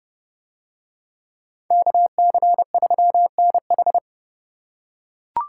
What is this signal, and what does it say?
Morse code, heard as a keyed tone, about 27 wpm, 700 Hz. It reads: KC3NH